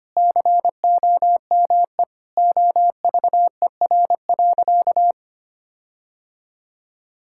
Morse code, heard as a keyed tone, about 25 wpm, 700 Hz. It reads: COME OVER.
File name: practice / 25wpm / Sentences with 4-Word Phrases - 25wpm